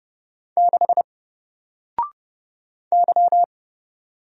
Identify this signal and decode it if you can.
Morse code, heard as a keyed tone, about 30 wpm, 700 Hz. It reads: 6 Y
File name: Single-Letter-Number-30wpm